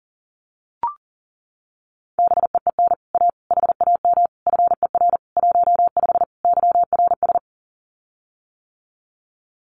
Morse code, heard as a keyed tone, about 41 wpm, 700 Hz. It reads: BEEN A HAM FER 15 YRS
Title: Common QSO Phrases Encore 40wpm